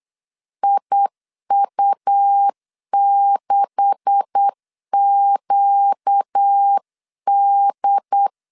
Morse code, heard as a keyed tone, about 8 wpm, 800 Hz. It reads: IU6QD
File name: CW / DET / 1 9 1500C S C 8wpm 5653